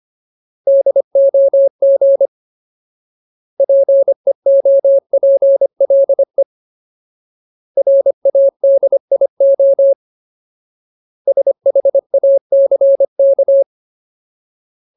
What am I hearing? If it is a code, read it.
Morse code, heard as a keyed tone, about 25 wpm, 550 Hz. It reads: DOG PEOPLE RADIO SHACK
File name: Ultimate Word List / 25WPM Ultimate